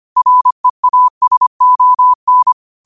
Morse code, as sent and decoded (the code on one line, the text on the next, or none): .-. .- ... --- -.
REASON